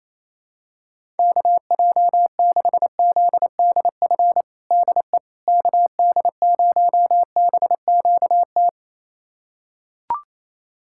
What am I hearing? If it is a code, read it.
Morse code, heard as a keyed tone, about 28 wpm, 700 Hz. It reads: KJ6ZDF DE KD0BQT
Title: USA Callsign DE Callsign - 6 Characters - 28wpm